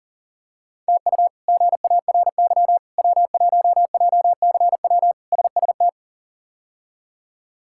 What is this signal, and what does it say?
Morse code, heard as a keyed tone, about 40 wpm, 700 Hz. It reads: TU GARY W1JCW SST
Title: K1USN SST QSO Practice - 40wpm